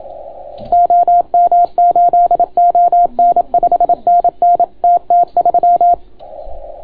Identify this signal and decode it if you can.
Morse code, heard as a keyed tone, about 27 wpm, 690 Hz. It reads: OM8ON 5NN TT3